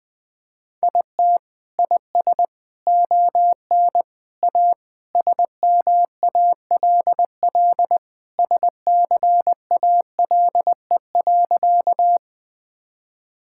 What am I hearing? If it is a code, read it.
Morse code, heard as a keyed tone, about 21 wpm, 700 Hz. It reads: IT IS ON A SMALL SCALE.
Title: Sentences from Top 1000 Words - Review Part 5 - 20wpm